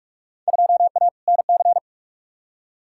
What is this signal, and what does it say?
Morse code, heard as a keyed tone, about 46 wpm, 700 Hz. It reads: 2A NC
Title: Field Day Exchanges - 45wpm